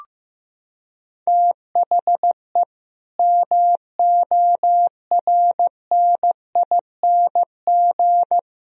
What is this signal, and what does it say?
Morse code, heard as a keyed tone, about 15 wpm, 700 Hz. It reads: THE MORNING
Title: Sentences from Top 1000 Words - Review Part 3 - 15wpm